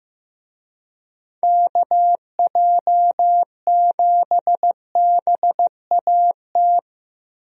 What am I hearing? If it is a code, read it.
Morse code, heard as a keyed tone, about 15 wpm, 700 Hz. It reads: KJ7BAT
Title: USA Call Signs - 6 Characters - Repeated 3x - 15wpm